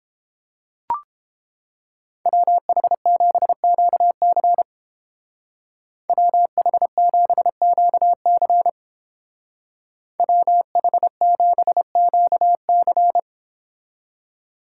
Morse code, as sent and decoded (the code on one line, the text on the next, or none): .-- .... --... --.- -.-. / .-- .... --... --.- -.-. / .-- .... --... --.- -.-.
WH7QC WH7QC WH7QC